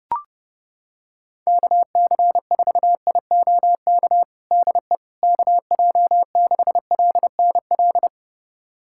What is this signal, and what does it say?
Morse code, heard as a keyed tone, about 30 wpm, 700 Hz. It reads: KC4IOK DE KJ6LNL
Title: USA Callsign DE Callsign - 6 Characters - 30wpm